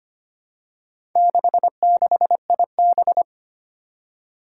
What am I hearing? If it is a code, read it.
Morse code, heard as a keyed tone, about 25 wpm, 700 Hz. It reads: 66IB
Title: Four Challenging Letter-Number Combinations 25wpm